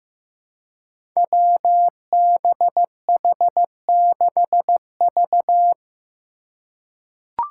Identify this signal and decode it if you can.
Morse code, heard as a keyed tone, about 15 wpm, 700 Hz. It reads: WBH6V
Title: Five Challenging Letter-Number Combinations 15wpm